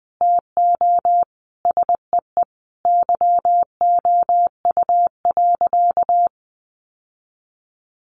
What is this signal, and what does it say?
Morse code, heard as a keyed tone, about 20 wpm, 700 Hz. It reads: TO SEE YOU.
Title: Sentences from Top 800 Words 20wpm